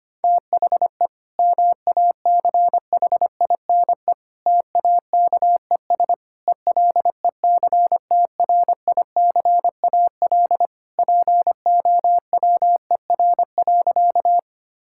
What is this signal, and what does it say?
Morse code, heard as a keyed tone, about 25 wpm, 700 Hz. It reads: THE MACHINE TAKES ELECTRICAL POWER.
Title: Sentences from Top 1000 Words - Review Part 6 - 25wpm